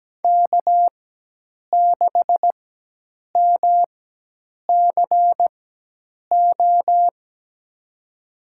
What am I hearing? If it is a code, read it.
Morse code, heard as a keyed tone, about 17 wpm, spaced as for 8 wpm, 700 Hz. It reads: K6MCO
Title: USA Call Signs - 5 Characters - 4x Character Spacing - 17wpm